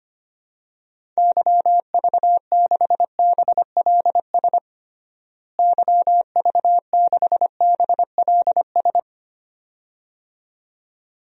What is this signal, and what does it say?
Morse code, heard as a keyed tone, about 25 wpm, 700 Hz. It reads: YV6BLS YV6BLS
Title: DXCC-100 Call Signs - 6 Characters - Repeated 2x - 25wpm